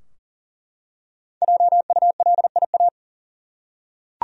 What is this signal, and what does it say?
Morse code, heard as a keyed tone, about 40 wpm, 700 Hz. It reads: JULIA
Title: USA Common Names - Max 5 Letters - 40wpm